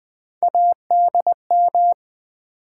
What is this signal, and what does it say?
Morse code, heard as a keyed tone, about 20 wpm, 700 Hz. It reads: ADM
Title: Build A Word - 1 Letter At A Time - 3 to 5-Letter Words - 20wpm